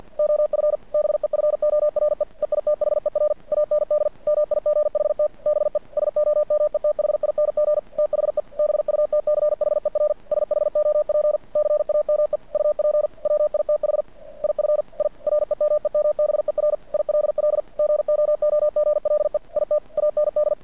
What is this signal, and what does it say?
Morse code, heard as a keyed tone, about 50 wpm, 590 Hz. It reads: OF BEFORE EITHER AND MIGHT BE SOMETHING THE BUTCHER SHOP CAME UP WITH IF I REMEMBER ILL GOOGLE IT AND
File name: archived / CRSnet